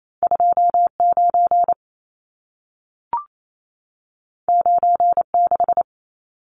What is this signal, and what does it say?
Morse code, heard as a keyed tone, about 28 wpm, 700 Hz. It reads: 29 96
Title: Numbers - 2-Digit 28wpm